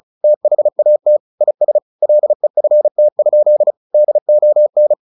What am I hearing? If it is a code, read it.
Morse code, heard as a keyed tone, about 34 wpm, 600 Hz. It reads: THAT IS LEFT? DON
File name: the old man and the sea / the old man and the sea 1 35wpm